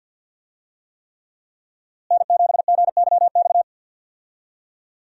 Morse code, heard as a keyed tone, about 50 wpm, 700 Hz. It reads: N7GYX
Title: USA Call Signs - 5 Characters - Repeated 3x - 50wpm